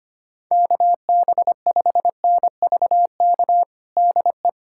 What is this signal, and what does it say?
Morse code, heard as a keyed tone, about 25 wpm, 700 Hz. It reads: KB5NVK DE
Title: USA Callsign DE Callsign 25wpm